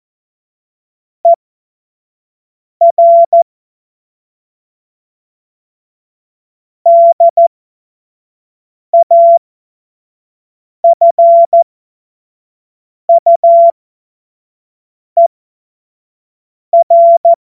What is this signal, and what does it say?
Morse code, heard as a keyed tone, about 14 wpm, spaced as for 5 wpm, 680 Hz. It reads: ER DAFUER